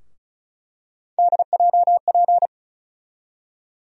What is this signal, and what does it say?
Morse code, heard as a keyed tone, about 36 wpm, 700 Hz. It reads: DJP
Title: USA Call Sign Postfixes - 3 Letters 35wpm